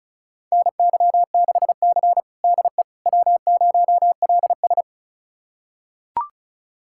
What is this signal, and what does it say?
Morse code, heard as a keyed tone, about 34 wpm, 700 Hz. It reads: NY6C DE W0LS
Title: USA Callsign DE Callsign - 4 Characters - 35wpm